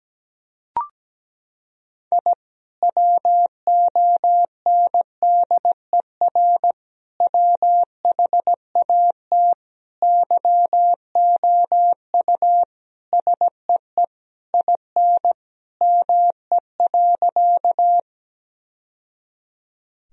Morse code, heard as a keyed tone, about 17 wpm, 700 Hz. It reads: I WONDER WHAT YOU SEE IN ME.
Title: Sentences from Top 1000 Words - Review Part 3 - 17wpm